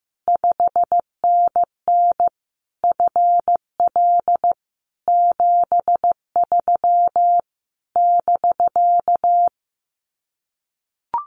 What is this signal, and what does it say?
Morse code, heard as a keyed tone, about 15 wpm, 700 Hz. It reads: 5NN FL 73 <BK>
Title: Common QSO Phrases Encore 15wpm